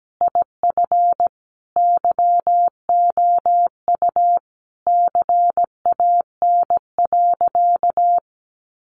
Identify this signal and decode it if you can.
Morse code, heard as a keyed tone, about 17 wpm, 700 Hz. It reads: IF YOU CAN.
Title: Sentences from Top 1100 Words 17wpm